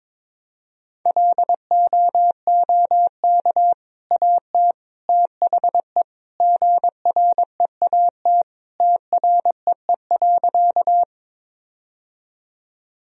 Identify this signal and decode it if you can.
Morse code, heard as a keyed tone, about 22 wpm, 700 Hz. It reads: LOOK AT THE GREAT TREE.
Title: Sentences from Top 1000 Words - Review Part 1 - 22wpm